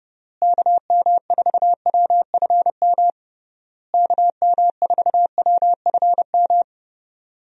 Morse code, heard as a keyed tone, about 32 wpm, 700 Hz. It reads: KM4WFM KM4WFM
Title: USA Call Signs - 6 Characters - Repeated 3x - 30wpm